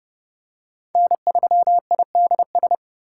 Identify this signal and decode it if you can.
Morse code, heard as a keyed tone, about 31 wpm, 700 Hz. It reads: N3IDS
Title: USA Call Signs - 5 Characters - 2x Character Spacing - 30wpm